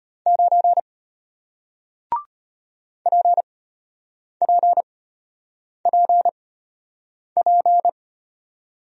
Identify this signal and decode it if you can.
Morse code, heard as a keyed tone, about 33 wpm, spaced as for 8 wpm, 700 Hz. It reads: 9 PPPP